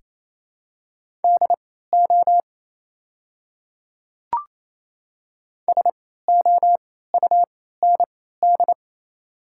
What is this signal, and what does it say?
Morse code, heard as a keyed tone, about 27 wpm, spaced as for 16 wpm, 690 Hz. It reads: DO SOUND
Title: 3x Character Spacing - Top 100 Words - 28wpm